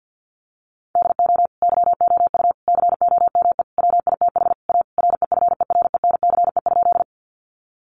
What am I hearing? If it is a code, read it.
Morse code, heard as a keyed tone, about 50 wpm, 700 Hz. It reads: DO YOU COME WITH A REFERENCE?
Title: Sentences from Top 1200 Words 50wpm